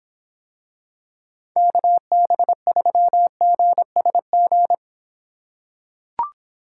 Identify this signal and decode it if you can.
Morse code, heard as a keyed tone, about 26 wpm, 700 Hz. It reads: KB3GSG